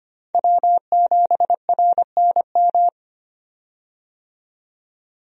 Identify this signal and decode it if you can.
Morse code, heard as a keyed tone, about 25 wpm, 700 Hz. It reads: W7RNM
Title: USA Call Signs - 5 Characters - 25wpm